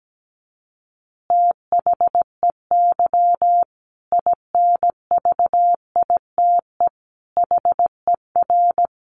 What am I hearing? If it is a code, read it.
Morse code, heard as a keyed tone, about 17 wpm, 700 Hz. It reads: THEY INVITE HER
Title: Easy Phrases 17wpm